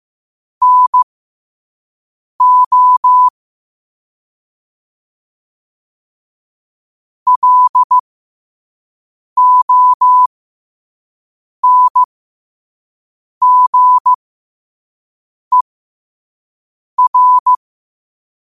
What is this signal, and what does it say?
Morse code, heard as a keyed tone, about 15 wpm, spaced as for 5 wpm, 1000 Hz. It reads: NO LONGER
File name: qotdfiles / 5wpm